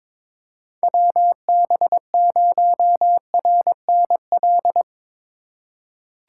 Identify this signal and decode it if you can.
Morse code, heard as a keyed tone, about 23 wpm, 700 Hz. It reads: WB0RNL
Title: USA Call Signs - 6 Characters - 4x Character Spacing - 22wpm